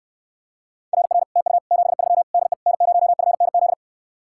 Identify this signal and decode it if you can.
Morse code, heard as a keyed tone, about 69 wpm, 700 Hz. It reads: RR TU 73 DE N0FNZ